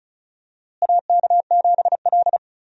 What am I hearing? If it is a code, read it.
Morse code, heard as a keyed tone, about 36 wpm, 700 Hz. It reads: AK7L
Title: USA Call Signs - 4 Characters - Repeated 3x - 35wpm